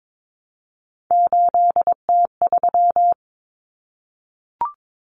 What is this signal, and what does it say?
Morse code, heard as a keyed tone, about 22 wpm, 700 Hz. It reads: 8T3
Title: Common Cut Numbers - 3-Digit 22wpm